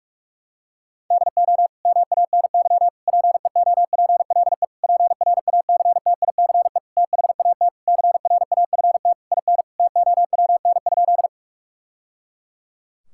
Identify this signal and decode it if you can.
Morse code, heard as a keyed tone, about 47 wpm, 700 Hz. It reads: DO MANY PEOPLE PRACTICE THAT CRAFT IN TOWN?
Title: Sentences from Top 1000 Words - Review Part 6 - 45wpm